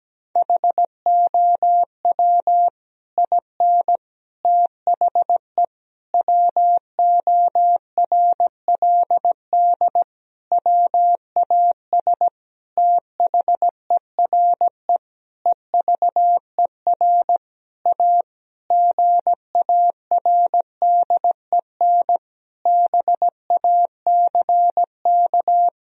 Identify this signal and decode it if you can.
Morse code, heard as a keyed tone, about 17 wpm, 700 Hz. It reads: HOW IN THE WORLD WAS THERE EVER A GARDEN BACK
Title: Sentences from Top 1000 Words - Review Part 4 - 17wpm